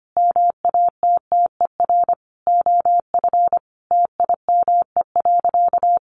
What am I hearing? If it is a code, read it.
Morse code, heard as a keyed tone, about 25 wpm, 700 Hz. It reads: MATTER OF TIME.
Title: Sentences from Top 1400 Words 25wpm